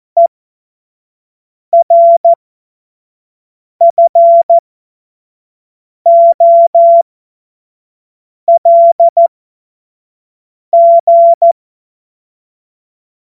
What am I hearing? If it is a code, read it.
Morse code, heard as a keyed tone, about 14 wpm, spaced as for 5 wpm, 680 Hz. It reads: ERFOLG
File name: Witz-08-Juli-2025-25-F70